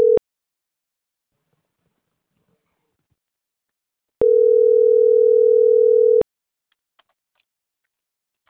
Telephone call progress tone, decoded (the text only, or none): ringback tone